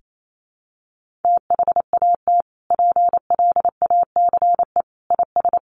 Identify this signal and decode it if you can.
Morse code, heard as a keyed tone, about 28 wpm, 700 Hz. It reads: THAT PLACE IS
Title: Sentences from Top 1500 Words 28wpm